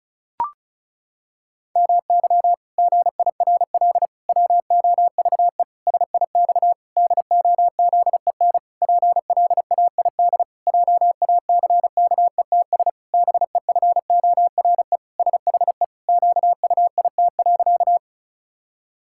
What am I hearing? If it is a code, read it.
Morse code, heard as a keyed tone, about 34 wpm, 700 Hz. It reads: MY GIRL WOVE SIX DOZEN PLAID JACKETS BEFORE SHE QUIT.
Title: Pangram Challenge v2 - 35wpm